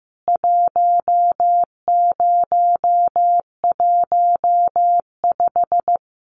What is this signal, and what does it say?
Morse code, heard as a keyed tone, about 15 wpm, 700 Hz. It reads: 1015
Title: CWT Exchange 15wpm